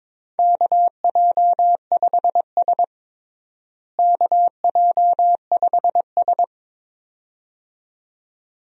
Morse code, heard as a keyed tone, about 22 wpm, 700 Hz. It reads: KJ5S KJ5S
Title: USA Call Signs - 4 Characters - Repeated 2x - 22wpm